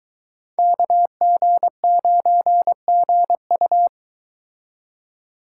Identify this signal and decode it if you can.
Morse code, heard as a keyed tone, about 22 wpm, 700 Hz. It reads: KG9GU